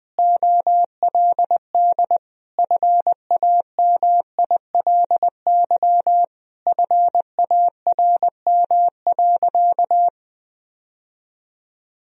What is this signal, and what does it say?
Morse code, heard as a keyed tone, about 20 wpm, 700 Hz. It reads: OLD FAMILY FARM.
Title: Sentences from Top 1000 Words - Review Part 2 - 20wpm